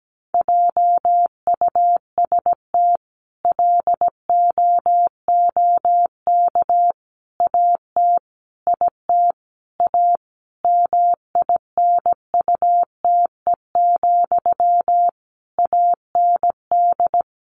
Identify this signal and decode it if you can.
Morse code, heard as a keyed tone, about 17 wpm, 700 Hz. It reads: JUST LOOK AT IT A MINUTE, AND
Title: Sentences from Top 1000 Words 17wpm